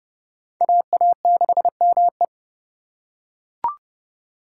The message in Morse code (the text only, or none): AA6ME